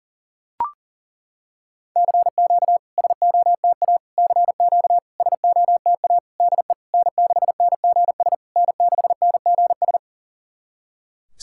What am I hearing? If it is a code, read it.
Morse code, heard as a keyed tone, about 40 wpm, 700 Hz. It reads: CQ SOTA CQ SOTA DE N6NGS N6NGS
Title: SOTA QSO Practice 40wpm